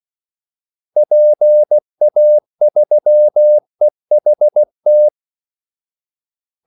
Morse code, heard as a keyed tone, about 16 wpm, 600 Hz. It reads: PA3EHT
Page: Kaldesignaler 16 wpm